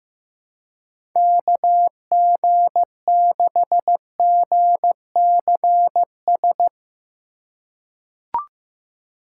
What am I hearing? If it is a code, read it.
Morse code, heard as a keyed tone, about 15 wpm, 700 Hz. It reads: KG6GCS
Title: USA Call Signs - 6 Characters - 2x Character Spacing - 15wpm